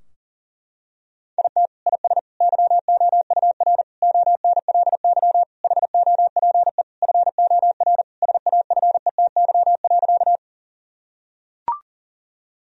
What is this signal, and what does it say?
Morse code, heard as a keyed tone, about 37 wpm, 700 Hz. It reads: IT IS YOUR ONLY HOPE FOR SAFETY.
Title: Sentences from Top 1000 Words - Review Part 4 - 40wpm